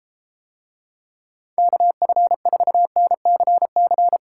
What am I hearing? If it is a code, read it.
Morse code, heard as a keyed tone, about 32 wpm, 700 Hz. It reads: KF4NCC